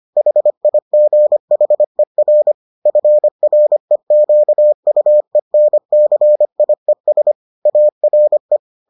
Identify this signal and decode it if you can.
Morse code, heard as a keyed tone, about 25 wpm, 600 Hz. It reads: HIGHER FREQUENCIES ARE